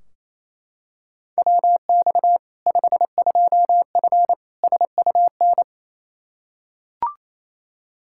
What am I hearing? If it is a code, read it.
Morse code, heard as a keyed tone, about 28 wpm, 700 Hz. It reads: WX 52F SUN
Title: Practice QSO - Exchange 2 - Info - 28wpm